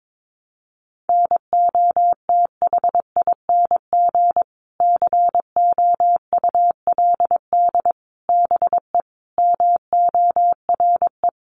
Morse code, heard as a keyed tone, about 22 wpm, 700 Hz. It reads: NOTHING COULD BE MORE
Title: Sentences from Top 900 Words 22wpm